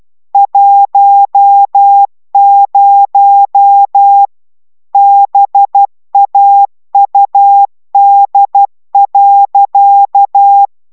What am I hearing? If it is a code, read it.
Morse code, heard as a keyed tone, about 12 wpm, 800 Hz. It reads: 10 BAUD.
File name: CW 10 baud